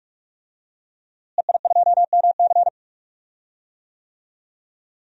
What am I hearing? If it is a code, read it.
Morse code, heard as a keyed tone, about 44 wpm, 700 Hz. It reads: EI2MC
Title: DXCC-100 Call Signs - 5 Characters - Repeated 2x - 45wpm